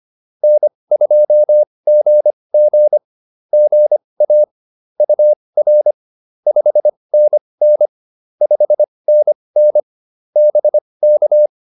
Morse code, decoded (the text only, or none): N2GG GA UR 5NN 5NN BK